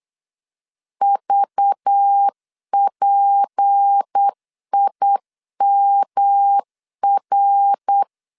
Morse code, heard as a keyed tone, about 8 wpm, 800 Hz. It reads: VPIMR
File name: LEZIONI / 6 PKX0 / 6 P 8wpm 3025